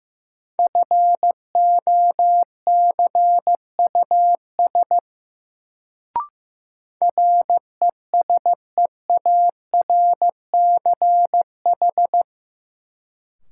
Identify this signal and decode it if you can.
Morse code, heard as a keyed tone, about 16 wpm, 700 Hz. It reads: FOCUS RESEARCH
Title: Sentences from Top 1000 Words - Review Part 4 - New Word Drill - 15wpm